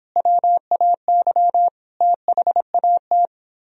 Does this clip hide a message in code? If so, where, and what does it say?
Yes: Morse code from 0.2 to 3.3 s, heard as a keyed tone, about 29 wpm, 700 Hz. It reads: WAY THAT